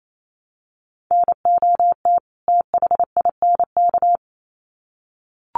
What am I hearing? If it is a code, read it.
Morse code, heard as a keyed tone, about 28 wpm, 700 Hz. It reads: NOT THINK
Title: Sets of 2 Words 28wpm